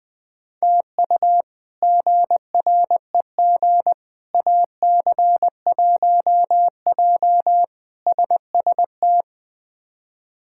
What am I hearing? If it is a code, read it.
Morse code, heard as a keyed tone, about 21 wpm, 700 Hz. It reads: TU GREG AC1J SST
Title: K1USN SST QSO Practice - 20wpm